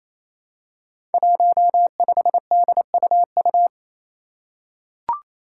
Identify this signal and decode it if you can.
Morse code, heard as a keyed tone, about 28 wpm, 700 Hz. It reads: 15DUU
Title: Five Challenging Letter-Number Combinations 28wpm